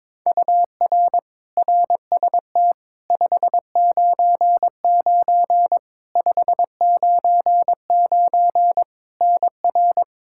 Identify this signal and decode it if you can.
Morse code, heard as a keyed tone, about 22 wpm, 700 Hz. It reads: UR RST 599 599 NR
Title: K1USN SST QSO Practice 22wpm